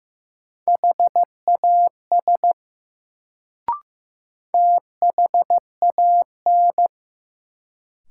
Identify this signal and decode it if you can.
Morse code, heard as a keyed tone, about 15 wpm, 700 Hz. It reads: HAS THAN
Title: Sentences from Top 100 Words - New Word Drill - 15wpm